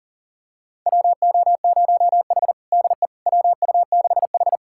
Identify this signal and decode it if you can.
Morse code, heard as a keyed tone, about 40 wpm, 700 Hz. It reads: WO0H DE WU6H
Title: USA Callsign DE Callsign - 4 Characters - 40wpm